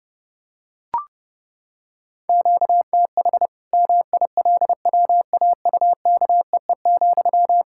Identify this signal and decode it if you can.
Morse code, heard as a keyed tone, about 30 wpm, 700 Hz. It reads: QTH MILWAUKEE,
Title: Practice QSO - Exchange 1 - Beginning - 30wpm